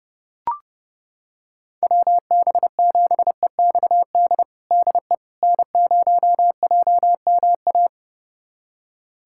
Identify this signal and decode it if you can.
Morse code, heard as a keyed tone, about 30 wpm, 700 Hz. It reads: WB7EXD DE N0JMA
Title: USA Callsign DE Callsign 30wpm